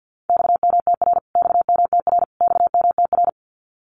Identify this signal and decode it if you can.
Morse code, heard as a keyed tone, about 50 wpm, 700 Hz. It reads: XMTR XMTR XMTR
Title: Abbreviations Repeated 3x - 50wpm